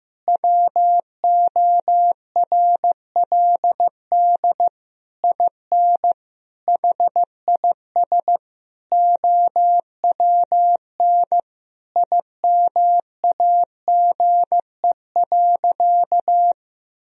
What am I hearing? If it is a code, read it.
Morse code, heard as a keyed tone, about 15 wpm, 700 Hz. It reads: WORLD IN HIS OWN IMAGE.
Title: Sentences from Top 1000 Words - Review Part 4 - 15wpm